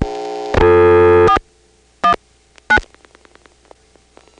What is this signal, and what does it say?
DTMF, heard as telephone touch tones, tones of about 90 ms. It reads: 419
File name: phonetrips / PrinceWmCoVA5HQ